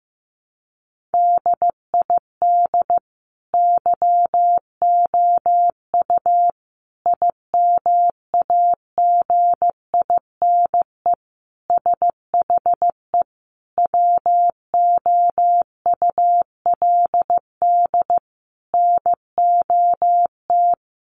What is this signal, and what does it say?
Morse code, heard as a keyed tone, about 15 wpm, 700 Hz. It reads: DID YOU IMAGINE SHE WOULD NOT
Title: Sentences from Top 1400 Words 15wpm